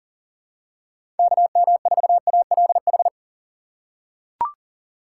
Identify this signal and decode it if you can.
Morse code, heard as a keyed tone, about 38 wpm, 700 Hz. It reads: KM4ALH
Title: USA Call Signs - 6 Characters - 4x Character Spacing - 40wpm